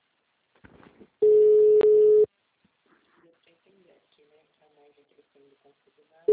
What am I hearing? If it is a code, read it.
Ringback tone (Europe, 425 Hz), with speech.
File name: out- -2043- - - .54543